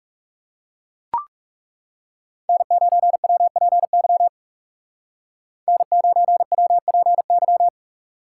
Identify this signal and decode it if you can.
Morse code, heard as a keyed tone, about 37 wpm, 700 Hz. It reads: N9WPY N9WPY
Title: Speed Racing - USA Call Signs 30wpm